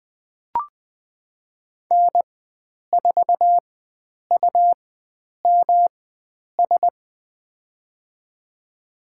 Morse code, heard as a keyed tone, about 20 wpm, spaced as for 9 wpm, 700 Hz. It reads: N4UMS
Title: USA Call Signs - 5 Characters - 4x Character Spacing - 20wpm